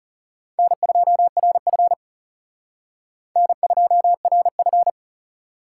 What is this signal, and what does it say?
Morse code, heard as a keyed tone, about 36 wpm, 700 Hz. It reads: N2RF N2RF
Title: Speed Racing - USA 1x2 Call Signs 30wpm